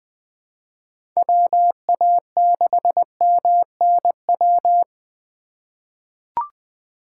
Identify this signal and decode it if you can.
Morse code, heard as a keyed tone, about 21 wpm, 700 Hz. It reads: WA6MNW
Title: USA Call Signs - 6 Characters - 3x Character Spacing - 20wpm